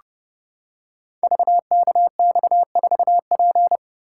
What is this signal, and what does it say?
Morse code, heard as a keyed tone, about 30 wpm, 700 Hz. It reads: VKX4P
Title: Code Groups - 5 Letter:Numbers - 30wpm